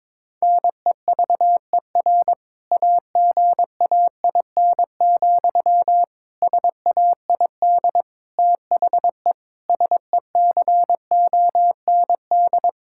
Morse code, heard as a keyed tone, about 22 wpm, 700 Hz. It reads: NEVER AGAIN, SAID THE SECOND
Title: Sentences from Top 1600 Words 22wpm